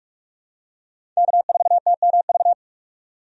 Morse code, heard as a keyed tone, about 46 wpm, 700 Hz. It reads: K4TMV
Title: USA Call Signs - 5 Characters - 45wpm